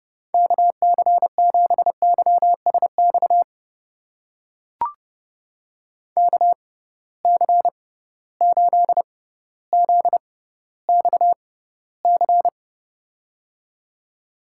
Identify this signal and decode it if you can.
Morse code, heard as a keyed tone, about 30 wpm, spaced as for 15 wpm, 700 Hz. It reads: KC7YSX KC8ZXC
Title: USA Call Signs - 6 Characters - 6x Character Spacing - 30wpm